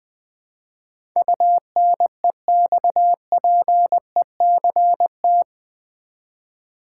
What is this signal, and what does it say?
Morse code, heard as a keyed tone, about 20 wpm, 700 Hz. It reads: UNEXPECT
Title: Build A Word - 4 Letters At A Time - 10-Letter Words - 20wpm